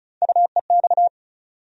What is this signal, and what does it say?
Morse code, heard as a keyed tone, about 36 wpm, 700 Hz. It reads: UEX
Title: USA Call Sign Postfixes - 3 Letters 35wpm